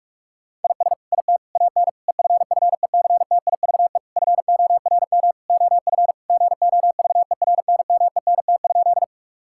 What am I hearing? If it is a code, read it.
Morse code, heard as a keyed tone, about 41 wpm, 700 Hz. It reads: IS IT AN EFFECTIVE FORM OF GOVERNMENT?
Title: Sentences from Top 1000 Words - Review Part 6 - 45wpm